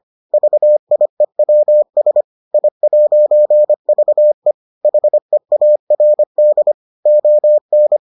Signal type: Morse code

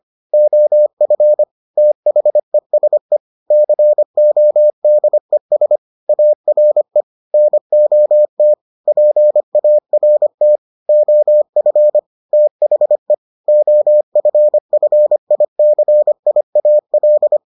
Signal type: Morse code

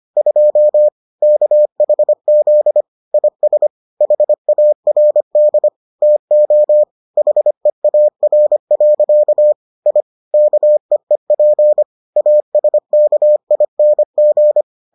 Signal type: Morse code

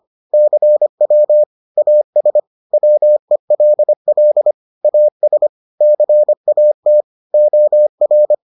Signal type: Morse code

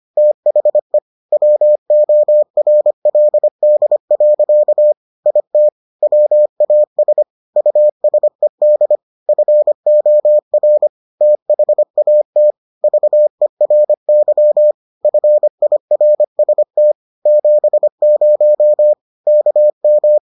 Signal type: Morse code

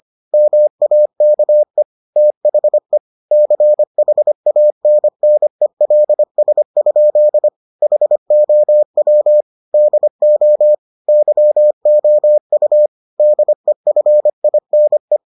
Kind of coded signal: Morse code

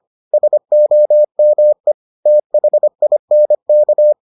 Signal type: Morse code

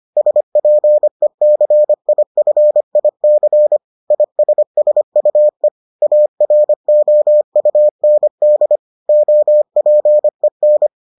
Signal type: Morse code

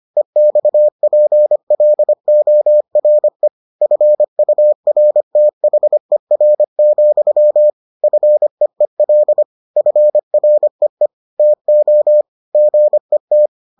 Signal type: Morse code

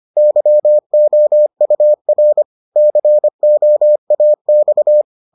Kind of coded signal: Morse code

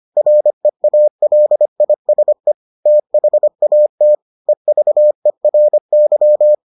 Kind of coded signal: Morse code

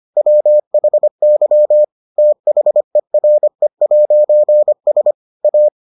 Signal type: Morse code

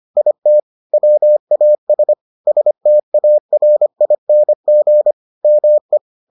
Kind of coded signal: Morse code

